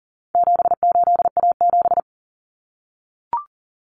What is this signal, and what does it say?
Morse code, heard as a keyed tone, about 38 wpm, 700 Hz. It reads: 78A7
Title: Common Cut Numbers - 4-Digit 40wpm